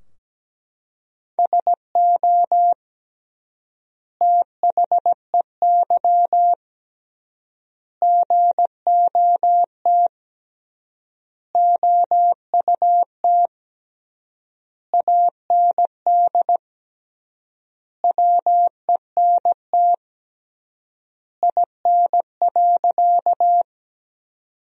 Morse code, heard as a keyed tone, about 17 wpm, 700 Hz. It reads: SO THEY GOT OUT AND WENT IN.
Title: 3x Word-Spacing - Sentences from Top 300 Words - 17wpm